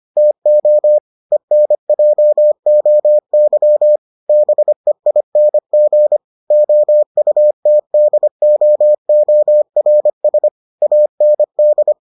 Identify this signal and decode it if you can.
Morse code, heard as a keyed tone, about 25 wpm, 600 Hz. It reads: TO ENJOY BEING OUTDOORS AND